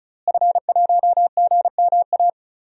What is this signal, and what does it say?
Morse code, heard as a keyed tone, about 36 wpm, 700 Hz. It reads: F1GMA
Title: DXCC-100 Call Signs - 5 Characters - Repeated 3x - 35wpm